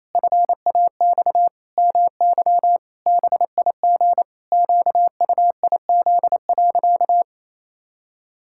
Morse code, heard as a keyed tone, about 28 wpm, 700 Hz. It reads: FAX MY BIG QUIZ.
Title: Pangram Challenge v2 - 28wpm